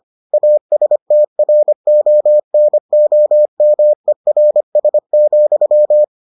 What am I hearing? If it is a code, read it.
Morse code, heard as a keyed tone, about 25 wpm, 600 Hz. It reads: ASTRONOMERS,